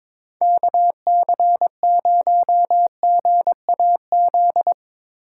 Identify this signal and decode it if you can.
Morse code, heard as a keyed tone, about 25 wpm, 700 Hz. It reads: KC0GAZ
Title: Speed Racing - USA 2x3 Call Signs 22wpm